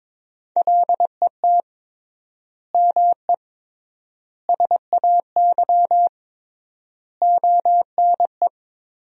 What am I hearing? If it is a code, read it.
Morse code, heard as a keyed tone, about 22 wpm, 700 Hz. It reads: LET ME SAY ONE
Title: 3x Word-Spacing - Sentences from Top 300 Words - 22wpm